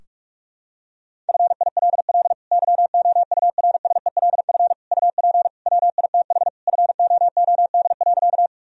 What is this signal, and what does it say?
Morse code, heard as a keyed tone, about 46 wpm, 700 Hz. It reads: FILL YOURSELF UP WITH FOOD.
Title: Sentences from Top 1000 Words - Review Part 3 - 45wpm